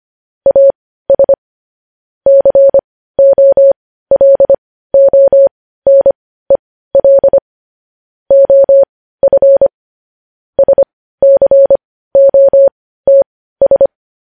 Morse code, heard as a keyed tone, about 25 wpm, spaced as for 15 wpm, 550 Hz. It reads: AS COLONEL OF SCOTS